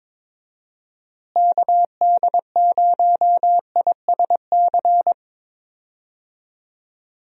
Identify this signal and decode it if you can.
Morse code, heard as a keyed tone, about 21 wpm, 700 Hz. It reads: KD0ISC
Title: USA Call Signs - 6 Characters - Repeated 3x - 22wpm